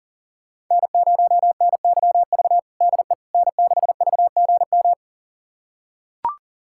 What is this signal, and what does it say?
Morse code, heard as a keyed tone, about 40 wpm, 700 Hz. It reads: N0NYV DE N6VGM
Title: USA Callsign DE Callsign - 5 Characters - 40wpm